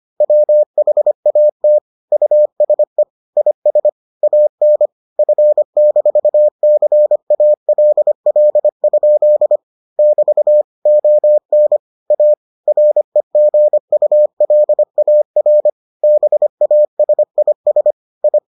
Morse code, heard as a keyed tone, about 25 wpm, 600 Hz. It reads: WHAT USE IS AN F-CALL? <BT> ON A REGULAR BASIS I